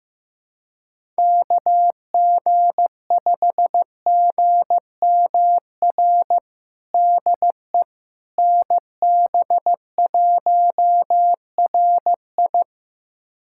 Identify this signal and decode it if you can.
Morse code, heard as a keyed tone, about 15 wpm, 700 Hz. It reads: KG5GMR DE NB1RI
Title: USA Callsign DE Callsign 15wpm